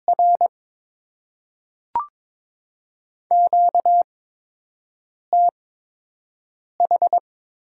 Morse code, heard as a keyed tone, about 22 wpm, spaced as for 6 wpm, 700 Hz. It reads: R QTH
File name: Top 20 QSO Elements - 8x Character Spacing - 22wpm